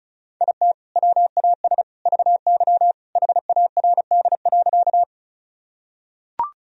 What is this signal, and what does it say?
Morse code, heard as a keyed tone, about 36 wpm, 700 Hz. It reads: IT WAS VY HARD.
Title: Sentences from Top 300 Words and Abbreviations 35wpm